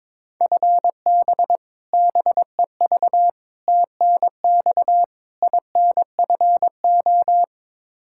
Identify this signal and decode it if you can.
Morse code, heard as a keyed tone, about 22 wpm, 700 Hz. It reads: FB BEV TNX INFO
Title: Practice QSO - Exchange 3 - More Info - 22wpm